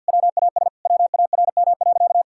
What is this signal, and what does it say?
Morse code, heard as a keyed tone, about 50 wpm, 700 Hz. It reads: WAS WARM.
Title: Sentences from Top 1000 Words - Review Part 5 - 50wpm